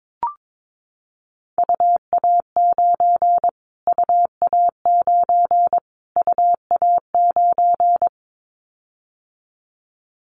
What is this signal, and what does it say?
Morse code, heard as a keyed tone, about 22 wpm, 700 Hz. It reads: UA9 UA9 UA9
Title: Callsign Prefixes for Top 25 DXCC - Repeated 3x - 22wpm